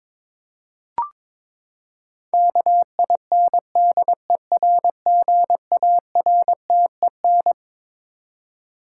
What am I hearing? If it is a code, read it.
Morse code, heard as a keyed tone, about 22 wpm, 700 Hz. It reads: KINDERGARTEN
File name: Twelve Letter Words - 22wpm